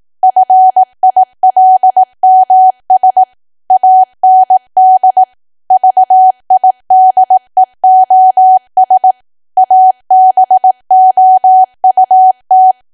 Morse code, heard as a keyed tone, about 18 wpm, 750 Hz. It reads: FILMS AND VIDEOS ABOUT
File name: Archive / 18 WPM / 18WPM